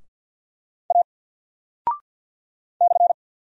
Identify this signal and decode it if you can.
Morse code, heard as a keyed tone, about 51 wpm, 700 Hz. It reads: A /
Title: Single Character (Letter:Number:Punctuation:Prosign) 50wpm